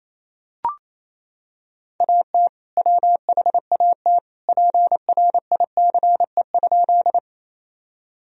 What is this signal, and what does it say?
Morse code, heard as a keyed tone, about 28 wpm, 700 Hz. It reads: AT WHAT PRICE?
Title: Sentences from Top 1000 Words - Review Part 4 - 28wpm